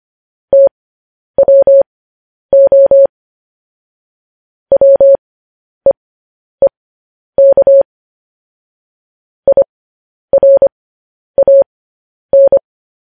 Morse code, heard as a keyed tone, about 25 wpm, spaced as for 10 wpm, 550 Hz. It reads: TWO WEEK IRAN
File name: podcast / News headlines-1x-25x10